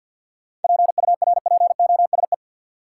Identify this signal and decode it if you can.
Morse code, heard as a keyed tone, about 49 wpm, 700 Hz. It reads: PURPOSE
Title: Sets of 5 Words - New Word Drill - 50wpm